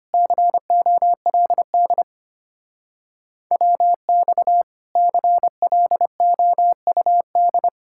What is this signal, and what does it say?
Morse code, heard as a keyed tone, about 33 wpm, 700 Hz. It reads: COLD WX CLOUD